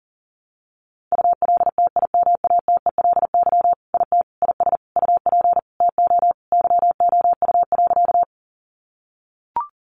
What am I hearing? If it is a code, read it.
Morse code, heard as a keyed tone, about 37 wpm, 700 Hz. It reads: ULTIMATELY IT IS UP TO YOU.
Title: Sentences from Top 1200 Words 40wpm